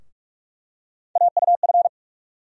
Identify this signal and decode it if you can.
Morse code, heard as a keyed tone, about 46 wpm, 700 Hz. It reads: AUF
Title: USA Call Sign Postfixes - 3 Letters 45wpm